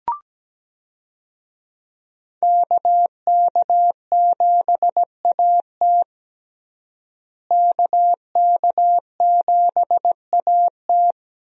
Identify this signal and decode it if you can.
Morse code, heard as a keyed tone, about 17 wpm, 700 Hz. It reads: KK7AT KK7AT
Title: USA Call Signs - 5 Characters - Repeated 3x - 17wpm